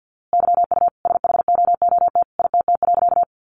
Morse code, heard as a keyed tone, about 50 wpm, 700 Hz. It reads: YU SHOOT ITT.
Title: A Sound Of Thunder 50wpm